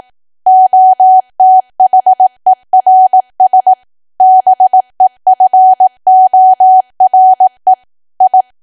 Morse code, heard as a keyed tone, about 18 wpm, 750 Hz. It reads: OTHERS BEFORE I
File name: Archive / 18 WPM / 18WPM